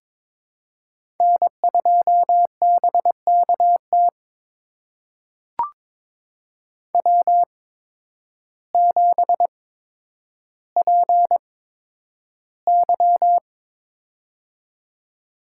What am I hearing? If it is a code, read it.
Morse code, heard as a keyed tone, about 22 wpm, spaced as for 10 wpm, 700 Hz. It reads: N2BKT W7PY
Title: USA Call Signs - 8x Character Spacing - 22wpm